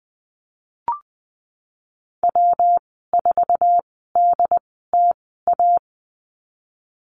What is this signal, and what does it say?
Morse code, heard as a keyed tone, about 20 wpm, spaced as for 15 wpm, 700 Hz. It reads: W4DTA
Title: 122 Lesson 20g - Number 3 - Call Signs 20wpm